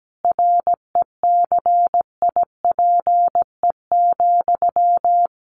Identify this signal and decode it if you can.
Morse code, heard as a keyed tone, about 17 wpm, 700 Hz. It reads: RECIPE,
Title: Sentences from Top 1200 Words 17wpm